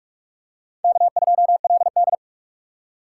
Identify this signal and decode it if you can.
Morse code, heard as a keyed tone, about 44 wpm, 700 Hz. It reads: K2LD
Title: USA Call Signs - 4 Characters - 2x Character Spacing - 45wpm